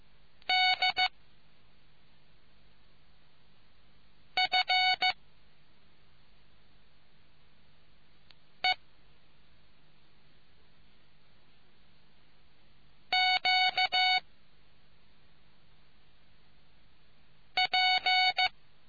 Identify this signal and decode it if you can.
Morse code, heard as a keyed tone, about 14 wpm, 740 Hz. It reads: D F E Q P